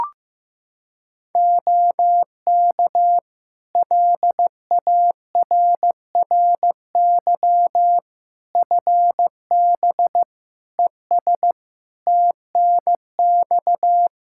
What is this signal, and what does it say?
Morse code, heard as a keyed tone, about 15 wpm, 700 Hz. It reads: OK LARRY FB ES TNX
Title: Practice QSO - Exchange 2 - Info - 15wpm